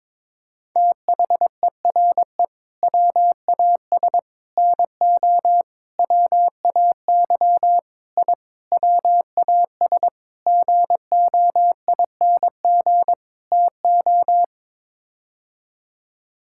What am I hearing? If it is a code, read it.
Morse code, heard as a keyed tone, about 22 wpm, 700 Hz. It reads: THERE WAS NO WAY I WAS GOING TO
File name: practice / 22wpm / Sets of 8 Words - 22wpm